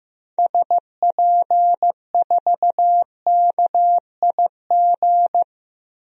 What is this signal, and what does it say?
Morse code, heard as a keyed tone, about 15 wpm, 700 Hz. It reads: SP4KIG
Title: DXCC-100 Call Signs - 6 Characters - 15wpm